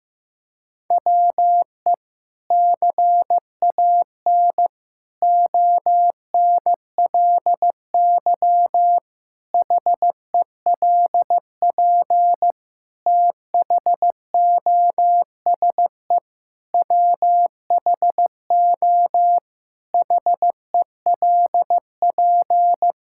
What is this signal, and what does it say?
Morse code, heard as a keyed tone, about 16 wpm, 700 Hz. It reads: WE CAN ONLY HELP THOSE WHO HELP
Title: Sentences from Top 1000 Words - Review Part 5 - 15wpm